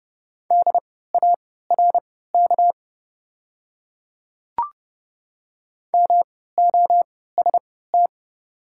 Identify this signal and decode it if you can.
Morse code, heard as a keyed tone, about 30 wpm, spaced as for 17 wpm, 690 Hz. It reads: DARK MOST
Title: 3x Character Spacing - Top 500 Words - 30wpm